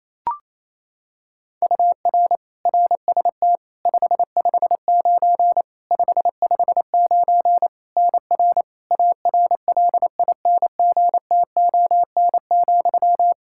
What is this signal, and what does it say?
Morse code, heard as a keyed tone, about 28 wpm, 700 Hz. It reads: UR RST 559 559 NR ARLINGTON,
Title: Practice QSO - Exchange 1 - Beginning - 28wpm